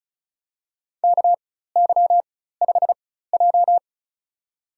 Morse code, heard as a keyed tone, about 34 wpm, spaced as for 16 wpm, 700 Hz. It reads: KY5J
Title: USA Call Signs - 4 Characters - 4x Character Spacing - 35wpm